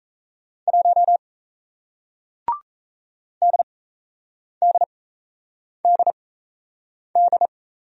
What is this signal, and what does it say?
Morse code, heard as a keyed tone, about 37 wpm, spaced as for 8 wpm, 700 Hz. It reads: 1 DDDD